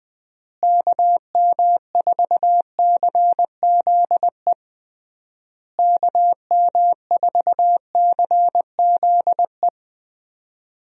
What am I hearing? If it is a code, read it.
Morse code, heard as a keyed tone, about 21 wpm, 700 Hz. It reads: KM4CZE KM4CZE
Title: USA Call Signs - 6 Characters - Repeated 3x - 20wpm